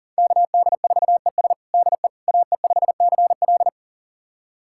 Morse code, heard as a keyed tone, about 37 wpm, 700 Hz. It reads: KD4ES DE AE5CL
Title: USA Callsign DE Callsign - 5 Characters - 40wpm